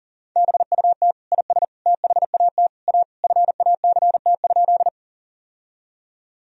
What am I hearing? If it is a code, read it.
Morse code, heard as a keyed tone, about 40 wpm, 700 Hz. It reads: BUT IS THAT A FACT?
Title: Sentences from Top 1000 Words - Review Part 3 - 40wpm